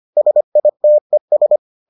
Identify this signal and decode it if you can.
Morse code, heard as a keyed tone, about 25 wpm, 600 Hz. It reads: SITES